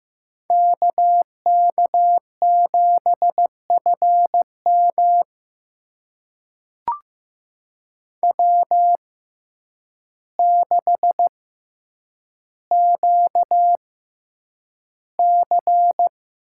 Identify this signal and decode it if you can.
Morse code, heard as a keyed tone, about 15 wpm, spaced as for 8 wpm, 700 Hz. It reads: KK7FM W6QC
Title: USA Call Signs - 5 Characters - 6x Character Spacing - 15wpm